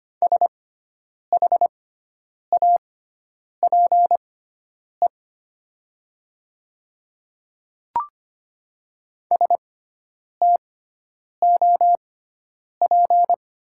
Morse code, heard as a keyed tone, about 25 wpm, spaced as for 9 wpm, 700 Hz. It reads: SHAPE STOP
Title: 6x Character Spacing - Top 500 Words - 25wpm